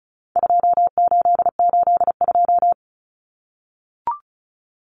Morse code, heard as a keyed tone, about 34 wpm, 700 Hz. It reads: 2882
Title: Common Cut Numbers - 4-Digit 35wpm